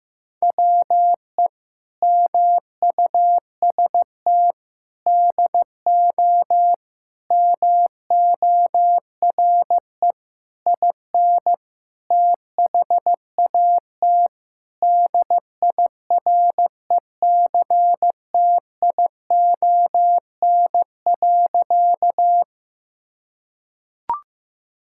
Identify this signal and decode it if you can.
Morse code, heard as a keyed tone, about 15 wpm, 700 Hz. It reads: WE MUST DO MORE IN THAT DIRECTION.
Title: Sentences from Top 1000 Words - Review Part 5 - 15wpm